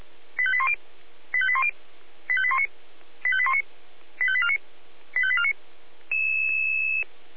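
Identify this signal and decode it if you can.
Five-tone selective call (ZVEI), heard as quick runs of five tones, tones of about 70 ms. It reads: 85810 85810 85819 85819 85849 85849
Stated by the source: ZVEI five-tone selective call received with an SDR.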